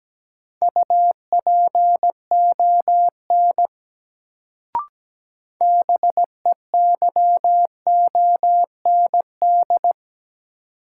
Morse code, heard as a keyed tone, about 17 wpm, 700 Hz. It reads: UPON BEYOND
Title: Sentences from Top 1000 Words - Review Part 6 - New Word Drill - 17wpm